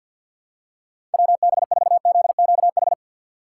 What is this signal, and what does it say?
Morse code, heard as a keyed tone, about 53 wpm, 700 Hz. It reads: WB4ZQH
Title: USA Call Signs - 6 Characters - 6x Character Spacing - 50wpm